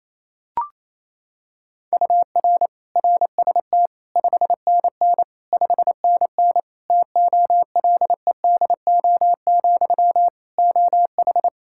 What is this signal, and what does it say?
Morse code, heard as a keyed tone, about 28 wpm, 700 Hz. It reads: UR RST 5NN 5NN TOLEDO, OH
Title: Practice QSO - Exchange 1 - Beginning - 28wpm